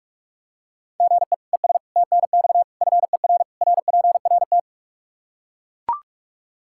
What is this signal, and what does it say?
Morse code, heard as a keyed tone, about 41 wpm, 700 Hz. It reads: GE ES TNX FER RPRT
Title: Practice QSO - Exchange 1 - Beginning - 45wpm